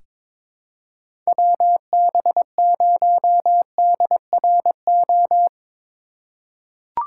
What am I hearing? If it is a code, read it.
Morse code, heard as a keyed tone, about 22 wpm, 700 Hz. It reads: WB0DRO
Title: USA Call Signs - 2x Character Spacing - 22wpm